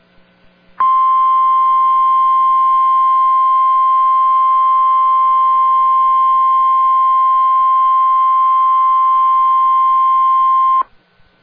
Emergency Alert System (SAME) tones (the 1050 Hz attention tone). It is heard from 0.7 to 10.8 s.